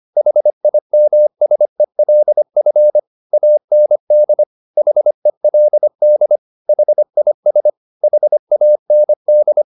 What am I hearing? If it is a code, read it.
Morse code, heard as a keyed tone, about 25 wpm, 600 Hz. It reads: HIMSELF AND HELD HIS HAND
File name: the old man and the sea part 1 25wpm